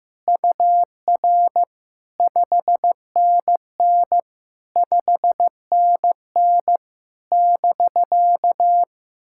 Morse code, heard as a keyed tone, about 16 wpm, 700 Hz. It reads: UR 5NN 5NN <BK>
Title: SOTA QSO Practice 15wpm